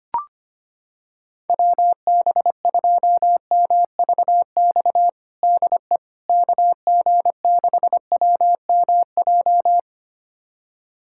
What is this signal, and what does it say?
Morse code, heard as a keyed tone, about 25 wpm, 700 Hz. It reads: WB2MVX DE KG6WMJ
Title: USA Callsign DE Callsign - 6 Characters - 25wpm